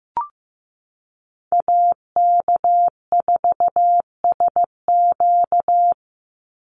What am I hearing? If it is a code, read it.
Morse code, heard as a keyed tone, about 15 wpm, 700 Hz. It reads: AK4SQ
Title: Call Signs Repeated 2x - 15wpm